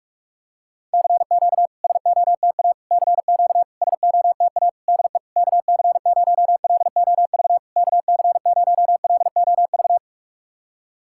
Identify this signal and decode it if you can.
Morse code, heard as a keyed tone, about 44 wpm, 700 Hz. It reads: CQ SOTA CQ SOTA DE KC0LOV KC0LOV
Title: SOTA QSO Practice 45wpm